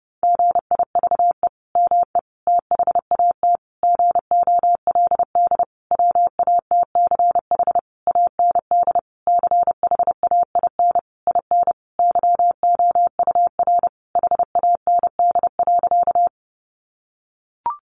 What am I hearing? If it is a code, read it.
Morse code, heard as a keyed tone, about 30 wpm, 700 Hz. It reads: GIVE ME THAT GOLD WATCH AND CHAIN IN YOUR HAND.
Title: Sentences from Top 1500 Words 30wpm